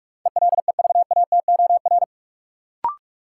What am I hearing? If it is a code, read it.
Morse code, heard as a keyed tone, about 43 wpm, 700 Hz. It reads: ELEVATOR
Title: Sentences from Top 1600 Words - New Word Drill - 45wpm